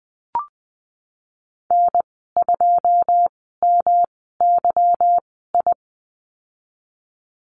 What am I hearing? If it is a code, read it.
Morse code, heard as a keyed tone, about 20 wpm, spaced as for 15 wpm, 700 Hz. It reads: N2MYI
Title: 173 Lesson 28g - Number 9 - Call Signs 20wpm